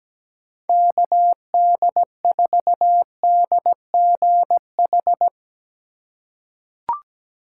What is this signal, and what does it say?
Morse code, heard as a keyed tone, about 17 wpm, 700 Hz. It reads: KD4DGH
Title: USA Call Signs - 6 Characters - 4x Character Spacing - 17wpm